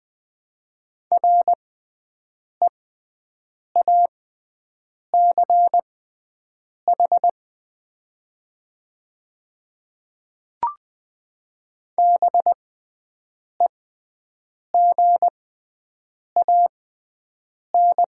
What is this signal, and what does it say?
Morse code, heard as a keyed tone, about 20 wpm, spaced as for 7 wpm, 700 Hz. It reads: REACH BEGAN
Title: 6x Character Spacing - Top 500 Words - 20wpm